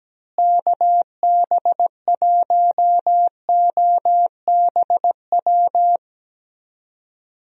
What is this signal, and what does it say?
Morse code, heard as a keyed tone, about 17 wpm, 700 Hz. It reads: KB1OBW
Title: USA Call Signs - 6 Characters - Repeated 3x - 17wpm